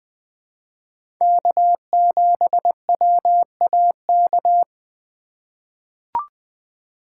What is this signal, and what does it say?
Morse code, heard as a keyed tone, about 20 wpm, 700 Hz. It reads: K7WAK
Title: USA Call Signs - 5 Characters - 4x Character Spacing - 20wpm